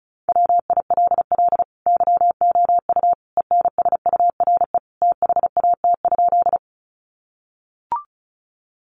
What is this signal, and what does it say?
Morse code, heard as a keyed tone, about 34 wpm, 700 Hz. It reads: WILL YOU ENSURE THAT?
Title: Sentences from Top 1100 Words 35wpm